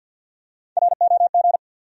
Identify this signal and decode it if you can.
Morse code, heard as a keyed tone, about 53 wpm, 700 Hz. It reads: ROG
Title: USA Common Names - 50wpm